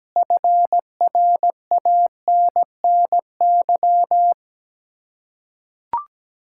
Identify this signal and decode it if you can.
Morse code, heard as a keyed tone, about 17 wpm, 700 Hz. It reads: FRANNY
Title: Practice QSO - Exchange 1 - Beginning - 17wpm